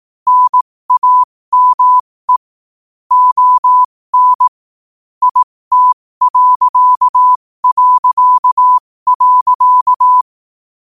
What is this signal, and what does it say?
Morse code, heard as a keyed tone, about 18 wpm, spaced as for 15 wpm, 1000 Hz. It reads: NAME ON IT...
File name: qotdfiles / 15wpm